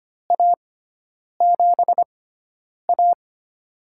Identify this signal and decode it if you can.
Morse code, heard as a keyed tone, about 25 wpm, spaced as for 9 wpm, 700 Hz. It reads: A7A
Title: USA Call Signs - 4 Characters - 6x Character Spacing - 25wpm